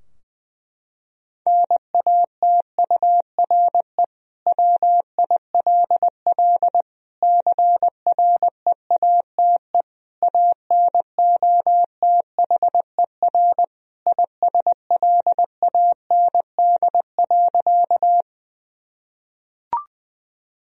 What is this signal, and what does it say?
Morse code, heard as a keyed tone, about 21 wpm, 700 Hz. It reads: NATURE WILL CREATE ANOTHER ISLAND.
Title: Sentences from Top 1000 Words - Review Part 3 - 20wpm